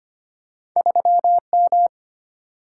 Morse code, heard as a keyed tone, about 25 wpm, 700 Hz. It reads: 3M
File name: Two Letter-Number Combinations - 25wpm